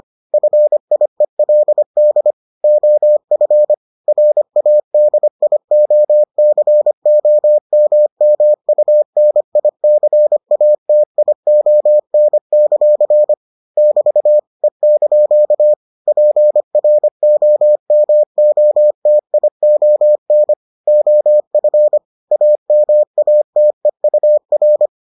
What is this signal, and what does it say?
Morse code, heard as a keyed tone, about 25 wpm, 600 Hz. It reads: FIELD OF RADIOCOMMUNICATION; <BT> E) PROMOTION OF AMATEUR